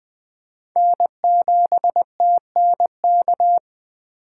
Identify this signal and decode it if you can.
Morse code, heard as a keyed tone, about 21 wpm, 700 Hz. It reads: N7TNK
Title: USA Call Signs - 5 Characters - 4x Character Spacing - 20wpm